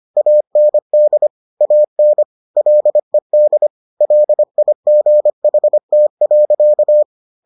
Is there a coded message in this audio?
Yes: Morse code from 0.2 to 7.0 s, heard as a keyed tone, about 25 wpm, 600 Hz. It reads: AND AN LED LIGHT.